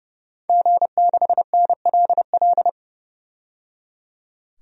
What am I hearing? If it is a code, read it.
Morse code, heard as a keyed tone, about 30 wpm, 700 Hz. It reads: G6NLL
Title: DXCC-100 Call Signs - 5 Characters - 30wpm